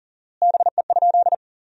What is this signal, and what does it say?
Morse code, heard as a keyed tone, about 40 wpm, 700 Hz. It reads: BE?
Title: 4x Word-Spacing - Sentences from Top 100 Words - 40wpm